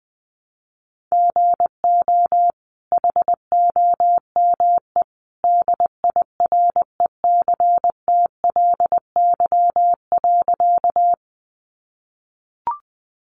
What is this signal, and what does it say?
Morse code, heard as a keyed tone, about 20 wpm, 700 Hz. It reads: GO HOME DIRECTLY.
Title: Sentences from Top 800 Words 20wpm